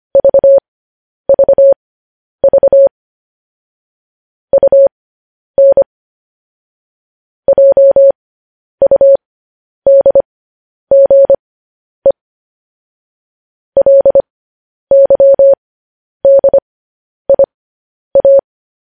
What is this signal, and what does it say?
Morse code, heard as a keyed tone, about 25 wpm, spaced as for 10 wpm, 550 Hz. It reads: VVV UN JUDGE LYDIA
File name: podcast / News headlines-1x-25x10